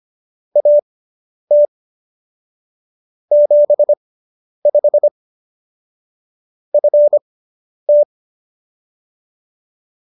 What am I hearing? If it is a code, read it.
Morse code, heard as a keyed tone, about 25 wpm, spaced as for 10 wpm, 590 Hz. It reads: AT 75 FT